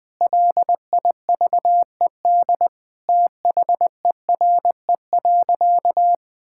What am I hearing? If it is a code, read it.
Morse code, heard as a keyed tone, about 20 wpm, 700 Hz. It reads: LIVED THERE.
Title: Sentences with 7-Word Phrases - 20wpm